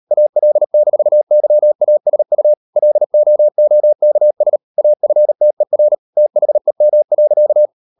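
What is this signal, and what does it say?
Morse code, heard as a keyed tone, about 38 wpm, 600 Hz. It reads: AL-YASU LOOKS AFTER THEM.
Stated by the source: Morse code practice at 38 WPM, sending Ken Follett's novel Never in English, from CW med Gnister.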